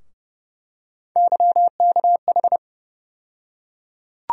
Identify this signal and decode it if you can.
Morse code, heard as a keyed tone, about 30 wpm, 700 Hz. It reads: YKH
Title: USA Call Sign Postfixes - 3 Letters 30wpm